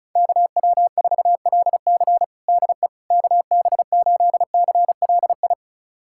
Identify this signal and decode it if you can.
Morse code, heard as a keyed tone, about 36 wpm, 700 Hz. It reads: KW4LC DE KB8CLI
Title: USA Callsign DE Callsign 35wpm